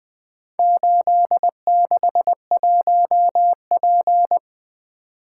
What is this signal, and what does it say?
Morse code, heard as a keyed tone, about 20 wpm, 700 Hz. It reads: 861P
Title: Four Letter-Number Combinations - 20wpm